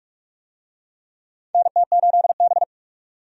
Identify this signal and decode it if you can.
Morse code, heard as a keyed tone, about 46 wpm, 700 Hz. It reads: NT8B
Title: USA Call Signs - 4 Characters - 45wpm